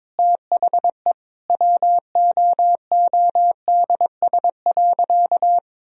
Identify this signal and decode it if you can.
Morse code, heard as a keyed tone, about 21 wpm, 700 Hz. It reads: THE WOODS.
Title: Sentences from Top 1000 Words - Review Part 2 - 22wpm